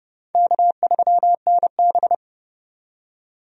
Morse code, heard as a keyed tone, about 30 wpm, 700 Hz. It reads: K3NB
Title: USA Call Signs - 4 Characters - 30wpm